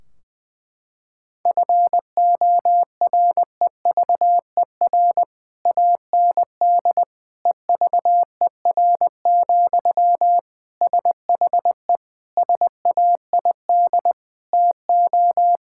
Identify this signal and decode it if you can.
Morse code, heard as a keyed tone, about 20 wpm, 700 Hz. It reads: FOREVER AND EVER, SHE SAID TO
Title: Sentences from Top 1600 Words 20wpm